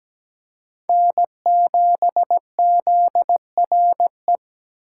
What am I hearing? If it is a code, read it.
Morse code, heard as a keyed tone, about 17 wpm, 700 Hz. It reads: N7ZRE
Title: USA Call Signs - 5 Characters - 8x Character Spacing - 17wpm